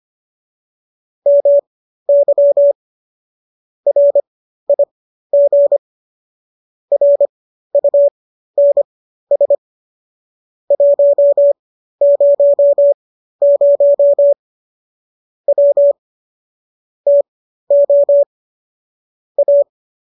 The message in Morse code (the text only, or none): MY RIG RUNS 100 W TO A